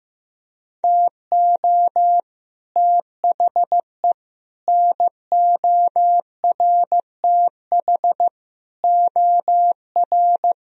Morse code, heard as a keyed tone, about 15 wpm, 700 Hz. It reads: TO THE NORTH OR
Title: Sentences from Top 1000 Words - Review Part 2 - 15wpm